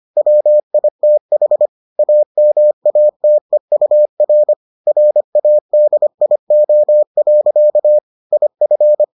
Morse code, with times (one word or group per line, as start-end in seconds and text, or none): 0.2-1.7 WITH
2.0-4.5 AMATEUR
4.9-8.0 RADIO.
8.3-9.0 IF